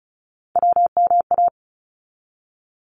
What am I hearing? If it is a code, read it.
Morse code, heard as a keyed tone, about 36 wpm, 700 Hz. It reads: WMA
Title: ARRL Sections 35wpm